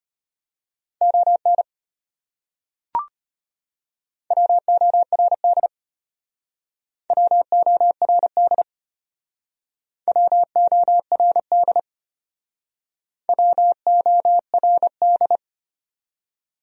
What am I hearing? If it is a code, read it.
Morse code, heard as a keyed tone, about 32 wpm, 700 Hz. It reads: ON WORD WORD WORD WORD